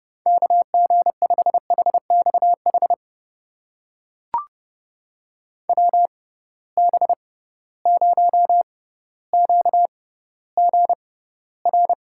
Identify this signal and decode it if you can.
Morse code, heard as a keyed tone, about 30 wpm, spaced as for 15 wpm, 700 Hz. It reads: KG5HXH WB0QGR
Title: USA Call Signs - 6 Characters - 6x Character Spacing - 30wpm